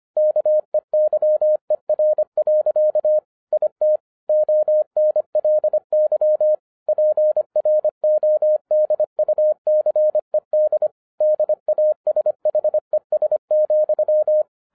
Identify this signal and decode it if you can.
Morse code, heard as a keyed tone, about 25 wpm, 610 Hz. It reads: KEYER. IT ONLY PRODUCED DASHES,
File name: FIRST FISTS CONTACT-25wpm